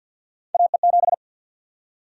Morse code, heard as a keyed tone, about 47 wpm, 700 Hz. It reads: AE7
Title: USA Call Sign Prefixes - 50wpm